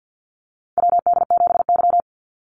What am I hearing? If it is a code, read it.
Morse code, heard as a keyed tone, about 50 wpm, 700 Hz. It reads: WD7Y
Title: NA SOTA Chaser Callsigns 50wpm